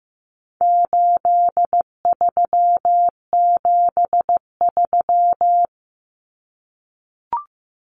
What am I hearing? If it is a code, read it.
Morse code, heard as a keyed tone, about 15 wpm, 700 Hz. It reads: 8373
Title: Numbers - 4-Digit 15wpm